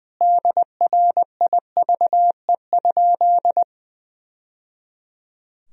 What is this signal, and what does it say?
Morse code, heard as a keyed tone, about 20 wpm, 700 Hz. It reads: DRIVE?
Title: Sentences from Top 1000 Words - Review Part 2 - 20wpm